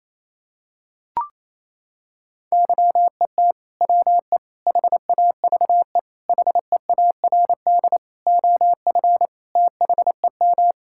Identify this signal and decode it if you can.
Morse code, heard as a keyed tone, about 28 wpm, 700 Hz. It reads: YET WE HAVE HEARD OF THEM
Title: Sentences from Top 1000 Words - Review Part 5 - 28wpm